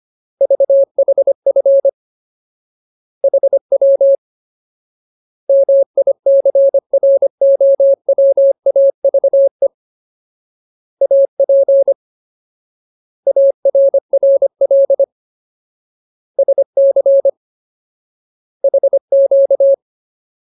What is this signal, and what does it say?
Morse code, heard as a keyed tone, about 25 wpm, 550 Hz. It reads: VHF HW MICROWAVE AP ARRL SC HQ
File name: Ultimate Word List / 25WPM Ultimate